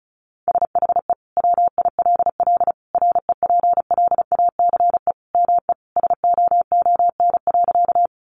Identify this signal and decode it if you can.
Morse code, heard as a keyed tone, about 34 wpm, 700 Hz. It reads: SHE WILL REPLACE ME SOON.
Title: Sentences from Top 1400 Words 35wpm